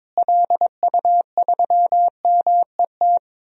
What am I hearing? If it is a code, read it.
Morse code, heard as a keyed tone, about 22 wpm, 700 Hz. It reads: LU3MET
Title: DXCC-100 Call Signs - 6 Characters - Repeated 3x - 22wpm